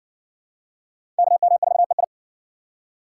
Morse code, heard as a keyed tone, about 58 wpm, 700 Hz. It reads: KM4EI